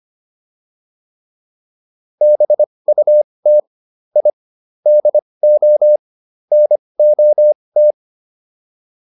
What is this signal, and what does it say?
Morse code, heard as a keyed tone, about 25 wpm, spaced as for 20 wpm, 600 Hz. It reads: BUT I DO NOT